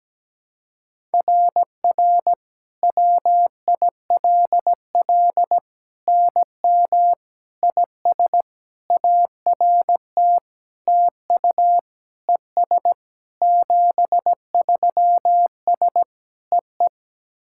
Morse code, heard as a keyed tone, about 17 wpm, 700 Hz. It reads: RR WILL NM IS ART TU ES 73S EE
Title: SOTA QSO Practice 17wpm